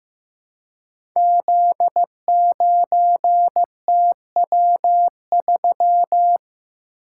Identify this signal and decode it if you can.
Morse code, heard as a keyed tone, about 15 wpm, 700 Hz. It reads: Z9TW3
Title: Code Groups - 5 Letter:Numbers - 15wpm